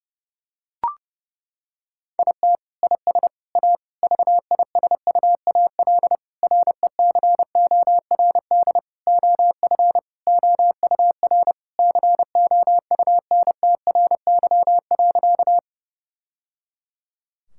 Morse code, heard as a keyed tone, about 30 wpm, 700 Hz. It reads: IT IS A VISUAL RECORD OF OUR COUNTRY.
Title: Sentences from Top 1000 Words - Review Part 6 - 30wpm